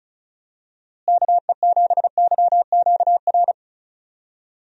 Morse code, heard as a keyed tone, about 36 wpm, 700 Hz. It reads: KE7YQR
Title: USA Call Signs - 6 Characters - Repeated 2x - 35wpm